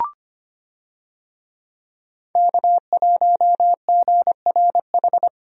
Morse code, heard as a keyed tone, about 25 wpm, 700 Hz. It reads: K1GRH
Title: USA Call Signs - 5 Characters - 25wpm